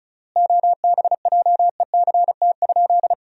Morse code, heard as a keyed tone, about 36 wpm, 700 Hz. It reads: OBJECT?
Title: 3x Word-Spacing - Sentences from Top 500 Words - 35wpm